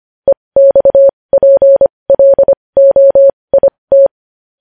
Morse code, heard as a keyed tone, about 25 wpm, spaced as for 20 wpm, 550 Hz. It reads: EXPLOIT
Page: News Headlines in Morse Code at 20 WPM